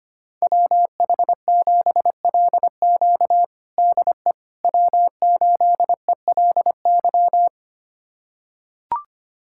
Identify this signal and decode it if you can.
Morse code, heard as a keyed tone, about 25 wpm, 700 Hz. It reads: WH7LQ DE W8ELY
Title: USA Callsign DE Callsign - 5 Characters - 25wpm